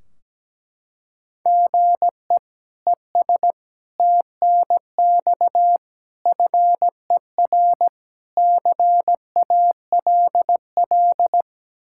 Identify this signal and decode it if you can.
Morse code, heard as a keyed tone, about 17 wpm, 700 Hz. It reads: GE ES TNX FER CALL
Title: K1USN SST QSO Practice 17wpm